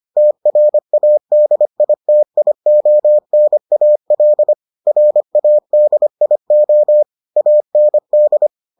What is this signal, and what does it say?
Morse code, heard as a keyed tone, about 25 wpm, 600 Hz. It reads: TRADITIONAL RADIO AND